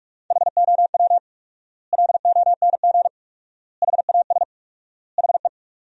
Morse code, heard as a keyed tone, about 46 wpm, 700 Hz. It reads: HOW LONG HAS HE
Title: 4x Word-Spacing - Sentences from Top 100 Words - 45wpm